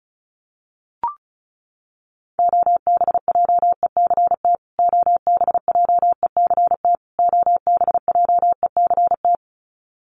Morse code, heard as a keyed tone, about 34 wpm, 700 Hz. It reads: OBJECT OBJECT OBJECT
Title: Top 500 Words Repeated 35wpm